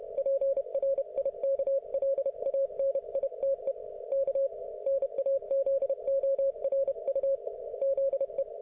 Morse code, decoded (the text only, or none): PRIKLANITE K NAZORU ZE